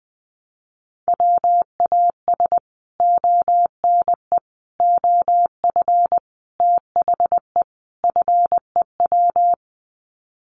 Morse code, heard as a keyed tone, about 20 wpm, 700 Hz. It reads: WAS ONE OF THE FEW
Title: Sets of 5 Words Encore 20wpm